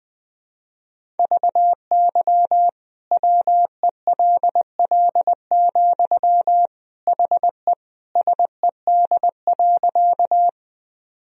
Morse code, heard as a keyed tone, about 20 wpm, 700 Hz. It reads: VY WELL, HE SED.
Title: Sentences from Top 200 Words and Abbreviations 20wpm